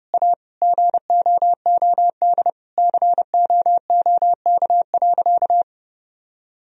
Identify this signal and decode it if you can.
Morse code, heard as a keyed tone, about 30 wpm, 700 Hz. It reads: A GOOD COOK.
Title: Sentences from Top 1000 Words - Review Part 4 - 30wpm